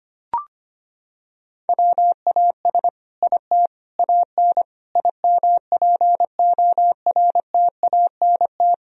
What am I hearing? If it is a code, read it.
Morse code, heard as a keyed tone, about 25 wpm, 700 Hz. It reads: WAS IT AN IMPORTANT
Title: Sentences from Top 1000 Words - Review Part 3 - 25wpm